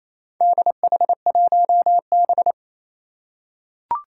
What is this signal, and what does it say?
Morse code, heard as a keyed tone, about 28 wpm, 700 Hz. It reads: DH1B
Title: Four Challenging Letter-Number Combinations 28wpm